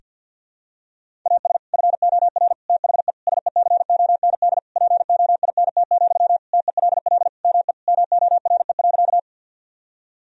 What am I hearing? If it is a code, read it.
Morse code, heard as a keyed tone, about 46 wpm, 700 Hz. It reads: AS FOR THE SECOND POINT, TELL ME MORE.
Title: Sentences from Top 1000 Words - Review Part 1 - 50wpm